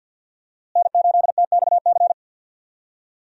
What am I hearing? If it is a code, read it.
Morse code, heard as a keyed tone, about 49 wpm, 700 Hz. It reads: N8TXC
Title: USA Call Signs - 5 Characters - Repeated 2x - 50wpm